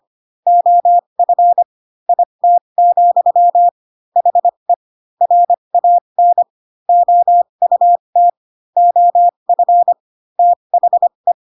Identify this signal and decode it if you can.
Morse code, heard as a keyed tone, about 25 wpm, spaced as for 22 wpm, 700 Hz. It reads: OF IT, HE RAN OUT OF THE
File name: Pinocchio / Pinocchio ch6 18 25 ik1hge